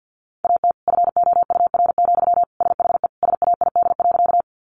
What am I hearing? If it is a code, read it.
Morse code, heard as a keyed tone, about 50 wpm, 700 Hz. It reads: AT FOUR, SHE SAID.
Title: Sentences from Top 300 Words 50wpm